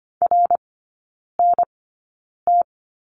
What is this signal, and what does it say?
Morse code, heard as a keyed tone, about 25 wpm, spaced as for 9 wpm, 700 Hz. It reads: RNT
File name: Lesson 17e - Letter u - Three Characters 25wpm